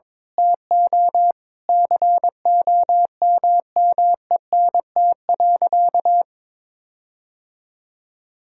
Morse code, heard as a keyed tone, about 22 wpm, 700 Hz. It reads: TO COMMENT.
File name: practice / 22wpm / Sentences with 8-Word Phrases - 22wpm